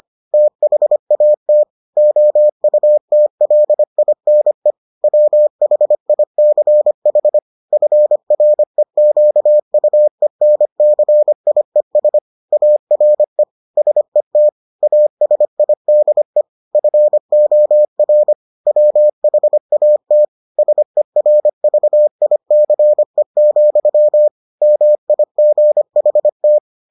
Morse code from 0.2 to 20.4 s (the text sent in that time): THAT OUTLINE WHICH FREQUENCIES ARE SET ASIDE FOR WHAT